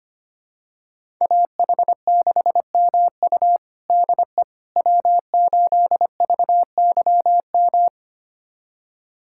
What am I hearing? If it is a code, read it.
Morse code, heard as a keyed tone, about 25 wpm, 700 Hz. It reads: AH6MU DE W8VYM
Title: USA Callsign DE Callsign - 5 Characters - 25wpm